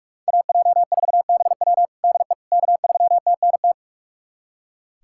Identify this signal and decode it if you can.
Morse code, heard as a keyed tone, about 42 wpm, 700 Hz. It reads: AJ4BW DE K3TNT
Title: USA Callsign DE Callsign - 5 Characters - 45wpm